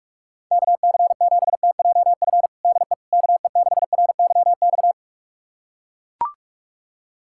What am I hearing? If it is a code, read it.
Morse code, heard as a keyed tone, about 41 wpm, 700 Hz. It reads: KC7TJF DE KE6RYX
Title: USA Callsign DE Callsign - 6 Characters - 45wpm